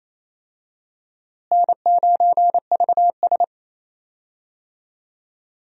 Morse code, heard as a keyed tone, about 28 wpm, 700 Hz. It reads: N9VS
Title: USA Call Signs - 4 Characters - Repeated 2x - 28wpm